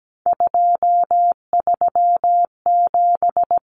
Morse code, heard as a keyed tone, about 17 wpm, 700 Hz. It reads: 237
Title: Numbers - 3-Digit 17wpm